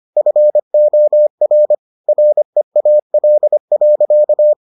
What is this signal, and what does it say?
Morse code, heard as a keyed tone, about 25 wpm, 600 Hz. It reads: FOR REAL.